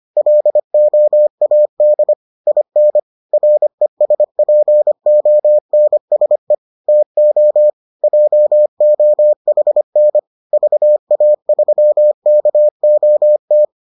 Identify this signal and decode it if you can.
Morse code, heard as a keyed tone, about 25 wpm, 600 Hz. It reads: LOAD IN RESPONSE TO JOHN VA3KOT